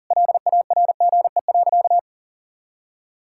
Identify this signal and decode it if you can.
Morse code, heard as a keyed tone, about 40 wpm, 700 Hz. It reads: LARGE.
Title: Sentences from Top 1000 Words - Review Part 5 - 40wpm